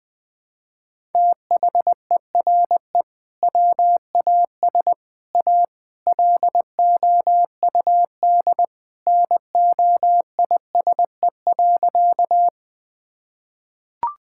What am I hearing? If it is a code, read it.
Morse code, heard as a keyed tone, about 20 wpm, 700 Hz. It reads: THERE WAS A LOUD NOISE.
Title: Sentences with 3-Word Phrases - 20wpm